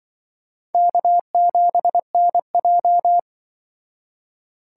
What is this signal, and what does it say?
Morse code, heard as a keyed tone, about 25 wpm, 700 Hz. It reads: K7NJ
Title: Speed Racing - USA 1x2 Call Signs 20wpm